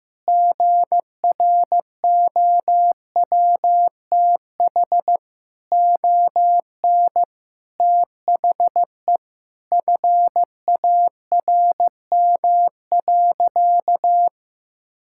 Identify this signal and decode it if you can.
Morse code, heard as a keyed tone, about 15 wpm, 700 Hz. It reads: GROWTH ON THE FARM.
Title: Sentences from Top 1000 Words - Review Part 5 - 15wpm